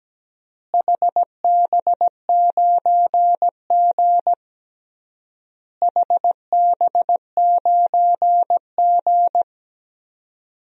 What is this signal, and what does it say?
Morse code, heard as a keyed tone, about 17 wpm, 700 Hz. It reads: HB9G HB9G
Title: DXCC-100 Call Signs - 4 Characters - Repeated 3x - 17wpm